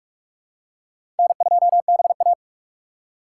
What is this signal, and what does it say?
Morse code, heard as a keyed tone, about 46 wpm, 700 Hz. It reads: N2BA
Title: USA Call Signs - 4 Characters - 6x Character Spacing - 45wpm